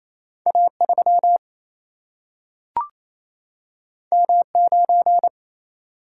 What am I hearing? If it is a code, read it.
Morse code, heard as a keyed tone, about 28 wpm, 700 Hz. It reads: A 3 M 9
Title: Single-Letter Single-Number 28wpm